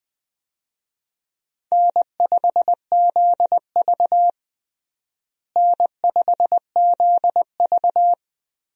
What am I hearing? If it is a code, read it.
Morse code, heard as a keyed tone, about 20 wpm, 700 Hz. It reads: N5ZV N5ZV
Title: USA Call Signs - 4 Characters - Repeated 3x - 20wpm